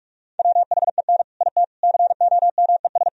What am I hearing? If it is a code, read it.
Morse code, heard as a keyed tone, about 60 wpm, spaced as for 52 wpm, 700 Hz. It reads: WHEN IT COMES
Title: Speed-Racing - Sets of 4 Words 45wpm